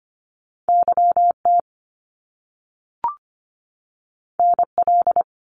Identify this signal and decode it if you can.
Morse code, heard as a keyed tone, about 25 wpm, 700 Hz. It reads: YT NL
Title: Canadian Provinces and Territories 25wpm